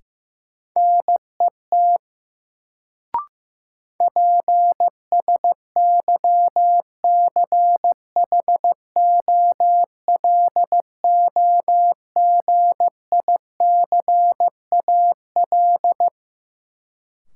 Morse code, heard as a keyed tone, about 15 wpm, 700 Hz. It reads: NET PSYCHOLOGICAL
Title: Sentences from Top 1400 Words - New Word Drill Alt - 15wpm